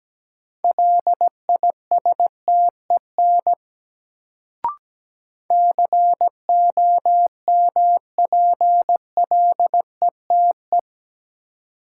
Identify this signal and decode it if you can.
Morse code, heard as a keyed tone, about 17 wpm, 700 Hz. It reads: LISTEN COMPLETE
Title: Sentences from Top 400 Words - New Word Drill - 17wpm